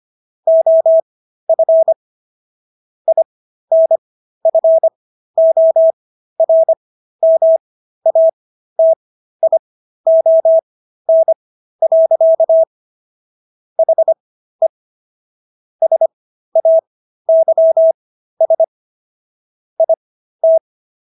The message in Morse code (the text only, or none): OF INFORMATION. HE SAYS IT